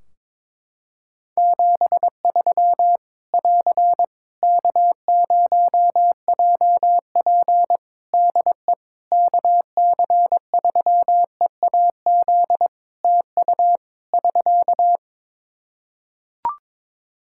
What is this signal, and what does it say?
Morse code, heard as a keyed tone, about 22 wpm, 700 Hz. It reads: 73 <AR> K0JP DE KC3EAZ TU <SK>
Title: Practice QSO - Exchange 4 - Ending - 22wpm